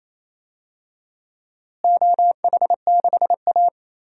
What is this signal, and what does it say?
Morse code, heard as a keyed tone, about 28 wpm, 700 Hz. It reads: OH6A
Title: DXCC-100 Call Signs - 4 Characters - Repeated 2x - 28wpm